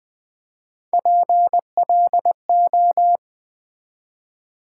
Morse code, heard as a keyed tone, about 20 wpm, 700 Hz. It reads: PLO
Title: Build A Word - 1 Letter At A Time - 3 to 5-Letter Words - 20wpm